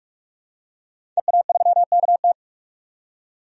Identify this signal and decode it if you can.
Morse code, heard as a keyed tone, about 41 wpm, 700 Hz. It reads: EA3KT
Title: DXCC-100 Call Signs - 5 Characters - 45wpm